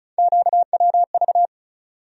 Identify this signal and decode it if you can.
Morse code, heard as a keyed tone, about 34 wpm, 700 Hz. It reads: QWV
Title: USA Call Sign Postfixes - 3 Letters 35wpm